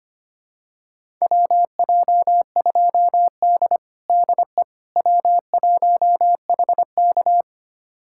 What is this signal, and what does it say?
Morse code, heard as a keyed tone, about 25 wpm, 700 Hz. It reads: WJ2D DE W1HK
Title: USA Callsign DE Callsign - 4 Characters - 25wpm